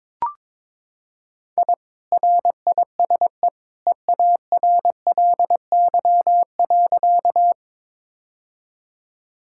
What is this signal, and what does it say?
Morse code, heard as a keyed tone, about 22 wpm, 700 Hz. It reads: I RISE EARLY.
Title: Sentences from Top 1000 Words - Review Part 6 - 22wpm